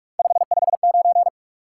Morse code, heard as a keyed tone, about 56 wpm, 700 Hz. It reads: 559